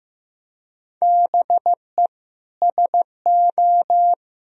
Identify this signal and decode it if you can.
Morse code, heard as a keyed tone, about 15 wpm, 700 Hz. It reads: BE SO
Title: Sentences from Top 1000 Words - Review Part 1 - 15wpm